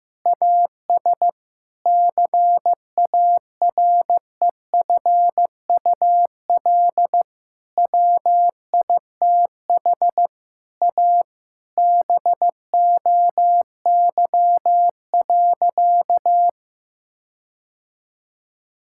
Morse code, heard as a keyed tone, about 15 wpm, 700 Hz. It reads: AS CAREFUL WITH A BOY.
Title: Sentences from Top 1600 Words 15wpm